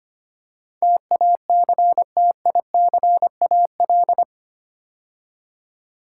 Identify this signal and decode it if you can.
Morse code, heard as a keyed tone, about 25 wpm, 700 Hz. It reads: TACTICAL
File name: 8 Letter Words - First 3 Letters Given - 25wpm